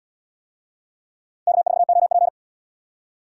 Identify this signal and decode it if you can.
Morse code, heard as a keyed tone, about 78 wpm, 700 Hz. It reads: K4OP